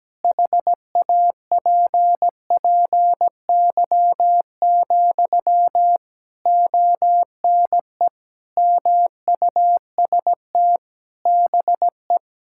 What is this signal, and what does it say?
Morse code, heard as a keyed tone, about 17 wpm, 700 Hz. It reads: HAPPY, ONE MUST BE
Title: Sentences from Top 1000 Words - Review Part 6 - 17wpm